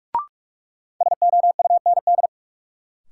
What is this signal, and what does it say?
Morse code, heard as a keyed tone, about 46 wpm, 700 Hz. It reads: SOUND
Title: Sets of 3 Words - New Word Drill - 45wpm